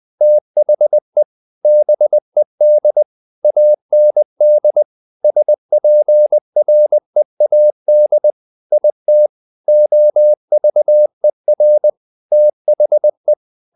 Morse code, heard as a keyed tone, about 20 wpm, 600 Hz. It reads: THE BED AND SPREAD IT OVER THE
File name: the old man and the sea / the old man and the sea part 1 20wpm